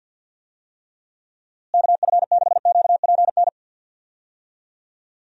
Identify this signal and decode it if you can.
Morse code, heard as a keyed tone, about 50 wpm, 700 Hz. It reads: KF6QPN
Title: USA Call Signs - 6 Characters - Repeated 2x - 50wpm